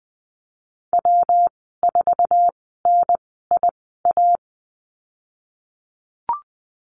Morse code, heard as a keyed tone, about 20 wpm, spaced as for 15 wpm, 700 Hz. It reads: W4NIA
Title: 179 Lesson 29g - Slash - Call Signs 20wpm